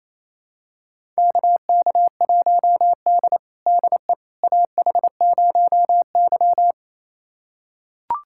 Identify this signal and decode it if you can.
Morse code, heard as a keyed tone, about 28 wpm, 700 Hz. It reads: KK1D DE AH0Y
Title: USA Callsign DE Callsign - 4 Characters - 28wpm